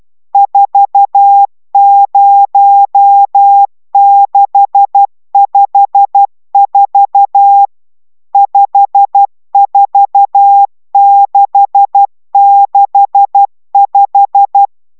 Morse code, heard as a keyed tone, about 12 wpm, 800 Hz. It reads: 40654 54665